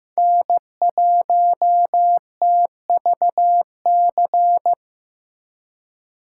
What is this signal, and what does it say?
Morse code, heard as a keyed tone, about 18 wpm, 700 Hz. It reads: N1TVC
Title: Speed Racing - USA 1x3 Call Signs 15wpm